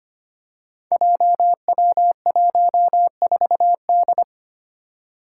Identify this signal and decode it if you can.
Morse code, heard as a keyed tone, about 25 wpm, 700 Hz. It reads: JW14D
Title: Five Challenging Letter-Number Combinations 25wpm